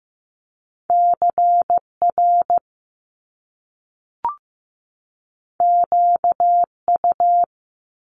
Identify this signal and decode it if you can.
Morse code, heard as a keyed tone, about 15 wpm, 700 Hz. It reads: CR QU
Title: Top Two Letter Combinations in Words 15wpm